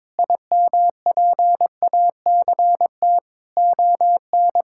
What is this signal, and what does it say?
Morse code, heard as a keyed tone, about 22 wpm, 700 Hz. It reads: IMPACT ON
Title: Sentences from Top 1000 Words - Review Part 6 - 22wpm